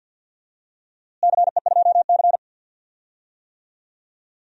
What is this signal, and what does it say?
Morse code, heard as a keyed tone, about 51 wpm, 700 Hz. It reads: CE2C
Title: DXCC-100 Call Signs - 4 Characters - Repeated 2x - 50wpm